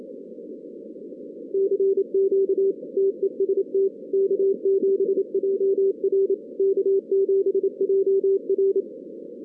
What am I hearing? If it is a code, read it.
Morse code, heard as a keyed tone, about 27 wpm, 390 Hz. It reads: CQ TEST K7JR K7JR
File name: 2016 arrl dx cw / 2016-ARRL-DX-CW-K5TR- - 0-39